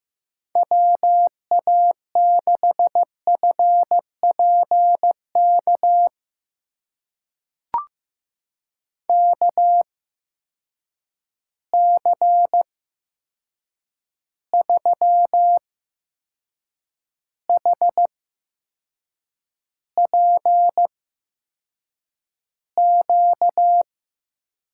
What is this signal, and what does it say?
Morse code, heard as a keyed tone, about 15 wpm, spaced as for 6 wpm, 700 Hz. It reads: WA6FPK KC3HPQ
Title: USA Call Signs - 6 Characters - 8x Character Spacing - 15wpm